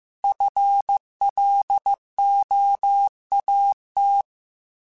Morse code, heard as a keyed tone, about 15 wpm, 800 Hz. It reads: FLOAT
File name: FL / FL mp3s / Fast FL @13